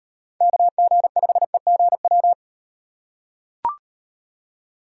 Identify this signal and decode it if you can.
Morse code, heard as a keyed tone, about 36 wpm, 700 Hz. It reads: KG5EGW